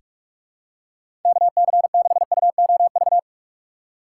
Morse code, heard as a keyed tone, about 43 wpm, 700 Hz. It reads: KC6UOV
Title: USA Call Signs - 6 Characters - 8x Character Spacing - 45wpm